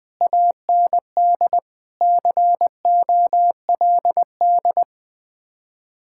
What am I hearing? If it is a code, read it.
Morse code, heard as a keyed tone, about 20 wpm, 700 Hz. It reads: AND COLD
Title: Speed-Racing Common QSO Phrases - 17wpm